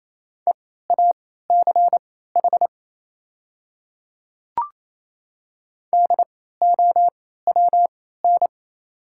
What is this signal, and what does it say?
Morse code, heard as a keyed tone, about 28 wpm, spaced as for 16 wpm, 690 Hz. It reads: EACH DOWN
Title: 3x Character Spacing - Top 100 Words - 28wpm